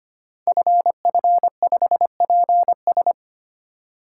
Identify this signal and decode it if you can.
Morse code, heard as a keyed tone, about 25 wpm, 700 Hz. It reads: FF5PS
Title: Code Groups - 5 Letter:Numbers - 25wpm